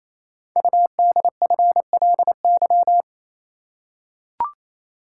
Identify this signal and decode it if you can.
Morse code, heard as a keyed tone, about 27 wpm, 700 Hz. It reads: UDFLY
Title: Code Groups - 5 Letters - 28wpm